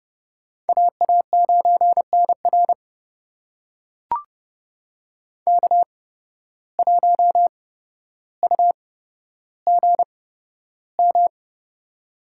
Morse code, heard as a keyed tone, about 30 wpm, spaced as for 13 wpm, 700 Hz. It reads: AA9NR K1UGM
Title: USA Call Signs - 5 Characters - 8x Character Spacing - 30wpm